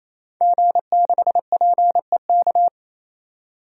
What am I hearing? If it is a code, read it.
Morse code, heard as a keyed tone, about 28 wpm, 700 Hz. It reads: G6PEK
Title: DXCC-100 Call Signs - 5 Characters - 28wpm